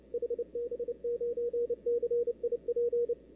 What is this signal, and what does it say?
Morse code, heard as a keyed tone, about 29 wpm, 470 Hz, with radio noise. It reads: HB9CIP
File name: CW / 2005-k5tr arrl cw dx -03:00